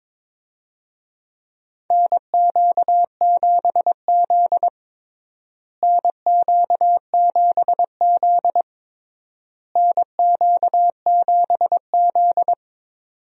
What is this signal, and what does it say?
Morse code, heard as a keyed tone, about 21 wpm, 700 Hz. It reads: NQ7Z NQ7Z NQ7Z
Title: USA Call Signs - 4 Characters - Repeated 3x - 22wpm